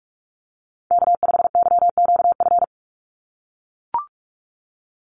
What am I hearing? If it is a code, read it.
Morse code, heard as a keyed tone, about 44 wpm, 700 Hz. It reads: K5YQF
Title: Call Signs - 45wpm